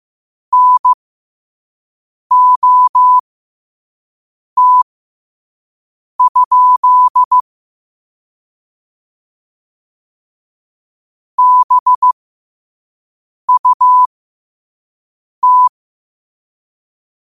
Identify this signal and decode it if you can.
Morse code, heard as a keyed tone, about 15 wpm, spaced as for 5 wpm, 1000 Hz. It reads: NOT? BUT
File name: qotdfiles / 5wpm